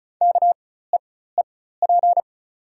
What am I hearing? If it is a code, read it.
Morse code, heard as a keyed tone, about 36 wpm, spaced as for 17 wpm, 700 Hz. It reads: KEEP
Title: 4x Character Spacing - Top 500 Words - 35wpm